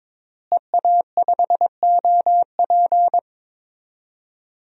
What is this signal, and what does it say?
Morse code, heard as a keyed tone, about 22 wpm, 700 Hz. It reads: EA5OP
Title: DXCC-100 Call Signs - 5 Characters - Repeated 2x - 22wpm